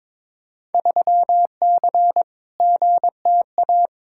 Morse code, heard as a keyed tone, about 22 wpm, 700 Hz. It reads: 3C GTA
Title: Field Day Exchanges Encore - 22wpm